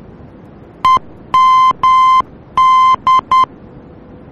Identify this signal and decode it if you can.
Morse code, heard as a keyed tone, about 10 wpm, 1040 Hz. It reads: EMD
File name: germany / mx / EMD-D